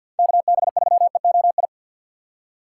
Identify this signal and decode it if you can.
Morse code, heard as a keyed tone, about 50 wpm, 700 Hz. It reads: KB3EOI
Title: USA Call Signs - 6 Characters - 6x Character Spacing - 50wpm